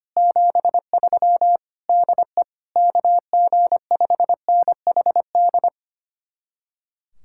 Morse code, heard as a keyed tone, about 31 wpm, 700 Hz. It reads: 73 DE KG5NHD
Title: Speed-Racing Common QSO Phrases - 25wpm